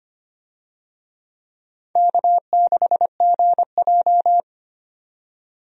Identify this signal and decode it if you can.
Morse code, heard as a keyed tone, about 25 wpm, 700 Hz. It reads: K6GJ
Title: USA Call Signs - 4 Characters - 25wpm